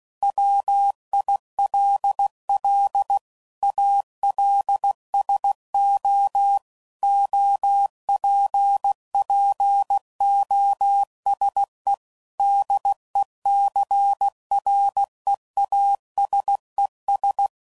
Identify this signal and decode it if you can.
Morse code, heard as a keyed tone, about 16 wpm, 800 Hz. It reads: WILL ALSO OPPOSE DECREASES
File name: CW Code Groups @15WPM / Technical Subjects / XL